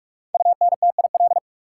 Morse code, heard as a keyed tone, about 41 wpm, 700 Hz. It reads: UNTIL
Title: Sentences from Top 500 Words - New Word Drill - 45wpm